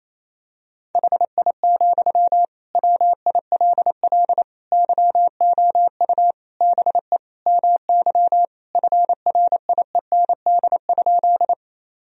Morse code, heard as a keyed tone, about 28 wpm, 700 Hz. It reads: HI, WILL YOU BE MY FRIEND?
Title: Sentences from Top 1000 Words - Review Part 3 - 28wpm